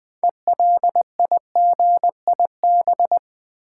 Morse code, heard as a keyed tone, about 20 wpm, 700 Hz. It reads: ELIGIB
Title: Build A Word - 3 Letters At A Time - 8-Letter Words - 20wpm